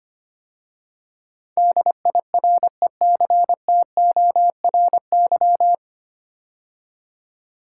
Morse code, heard as a keyed tone, about 25 wpm, 700 Hz. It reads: DIRECTORY
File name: practice / 25wpm / 9 Letter Words - First 3 Letters Given - 25wpm